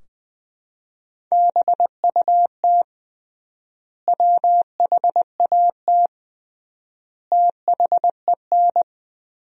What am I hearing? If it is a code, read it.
Morse code, heard as a keyed tone, about 20 wpm, 700 Hz. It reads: BUT WHAT THEN
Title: 3x Word-Spacing - Sentences from Top 500 Words - 20wpm